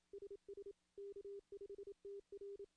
Morse code, heard as a keyed tone, about 28 wpm, 400 Hz. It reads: SS K5TR
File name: contests / CW-SS-k5tr- -1600